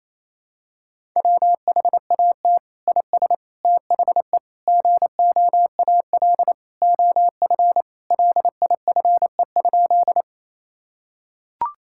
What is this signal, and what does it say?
Morse code, heard as a keyed tone, about 28 wpm, 700 Hz. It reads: WHAT IS THE GOAL OF LIFE?
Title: Sentences from Top 1000 Words - Review Part 5 - 28wpm